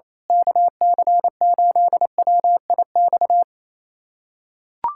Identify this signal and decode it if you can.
Morse code, heard as a keyed tone, about 27 wpm, 700 Hz. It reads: KC8WIX
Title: USA Callsign DE Callsign 28wpm